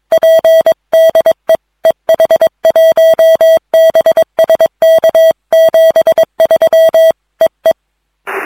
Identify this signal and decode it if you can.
Morse code, heard as a keyed tone, about 22 wpm, 650 Hz. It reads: P DE EH1BSK 73 EE